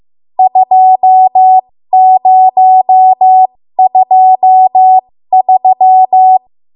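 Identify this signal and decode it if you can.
Morse code, heard as a keyed tone, about 15 wpm, spaced as for 13 wpm, 750 Hz. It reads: 2023
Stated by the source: 13 wpm Morse code practice archive transmission.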